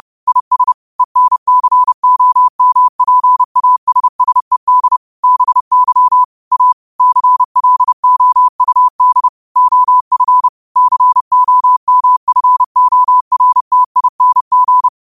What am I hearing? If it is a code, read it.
Morse code, heard as a keyed tone, about 30 wpm, 1000 Hz. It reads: IS ENCOMPASSED BY A CLOUD OF COMFORTING